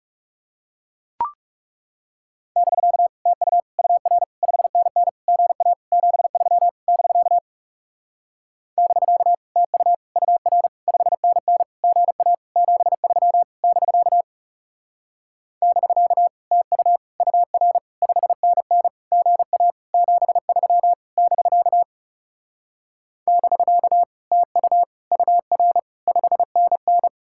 Morse code, heard as a keyed tone, about 40 wpm, 700 Hz. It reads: <BK> TU UR 5NN GA 73 <BK> <BK> TU UR 5NN GA 73 <BK> <BK> TU UR 5NN GA 73 <BK> <BK> TU UR 5NN